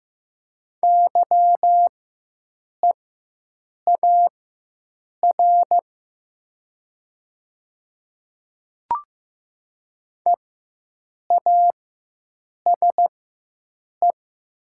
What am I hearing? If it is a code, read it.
Morse code, heard as a keyed tone, about 15 wpm, spaced as for 7 wpm, 700 Hz. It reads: YEAR EASE
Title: 4x Character Spacing - Top 300 Words - 15wpm